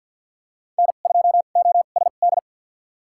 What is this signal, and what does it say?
Morse code, heard as a keyed tone, about 52 wpm, spaced as for 37 wpm, 700 Hz. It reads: N2OSD
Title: USA Call Signs - 5 Characters - 2x Character Spacing - 50wpm